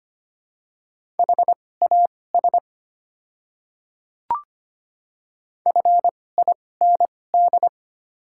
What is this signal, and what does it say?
Morse code, heard as a keyed tone, about 25 wpm, spaced as for 18 wpm, 700 Hz. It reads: HAS FIND
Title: 2x Character Spacing - Top 100 Words - 25wpm